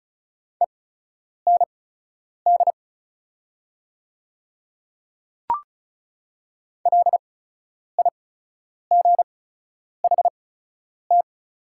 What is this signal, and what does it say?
Morse code, heard as a keyed tone, about 36 wpm, spaced as for 10 wpm, 700 Hz. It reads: END LIGHT
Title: 8x Character Spacing - Top 200 Words - 35wpm